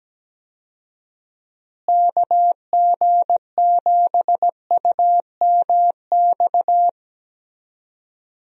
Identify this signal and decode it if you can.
Morse code, heard as a keyed tone, about 17 wpm, 700 Hz. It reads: KG7UMX
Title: USA Call Signs - 6 Characters - Repeated 3x - 17wpm